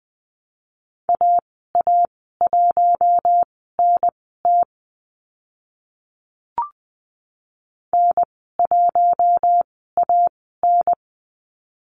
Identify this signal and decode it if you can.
Morse code, heard as a keyed tone, about 20 wpm, spaced as for 14 wpm, 700 Hz. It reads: AA1NT N1AN
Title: 042 Lesson 8g - Number 1 - Call Signs 20wpm